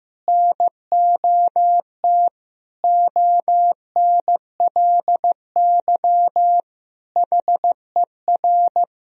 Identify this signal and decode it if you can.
Morse code, heard as a keyed tone, about 15 wpm, 700 Hz. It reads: NOT ONLY HER
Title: Sentences from Top 1000 Words - Review Part 2 - 15wpm